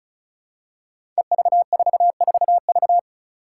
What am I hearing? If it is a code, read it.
Morse code, heard as a keyed tone, about 36 wpm, 700 Hz. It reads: EV44V
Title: Five Challenging Letter-Number Combinations 35wpm